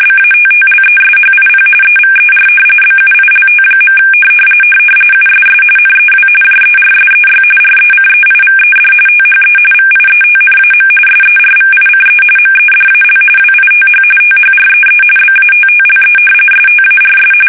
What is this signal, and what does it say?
RTTY, about 45.45 baud. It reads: BATTLESHIP EVER BUILT. USS NEW JERSEY RECEIVED THE VIETNAM SERVICE MEDAL WITH THREE BATTLE STARS
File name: fsk850 / battleshipnj